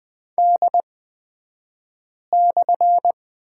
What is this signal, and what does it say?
Morse code, heard as a keyed tone, about 20 wpm, spaced as for 5 wpm, 700 Hz. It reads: D/
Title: 254 Lesson 41d - Period - Two Characters 20wpm